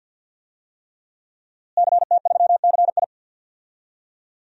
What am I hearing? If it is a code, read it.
Morse code, heard as a keyed tone, about 48 wpm, 700 Hz. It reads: CT3CI
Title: DXCC-100 Call Signs - 5 Characters - Repeated 3x - 50wpm